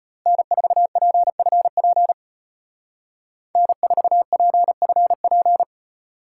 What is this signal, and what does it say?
Morse code, heard as a keyed tone, about 30 wpm, 700 Hz. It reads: N4PFP N4PFP
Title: Speed Racing - USA Call Signs 25wpm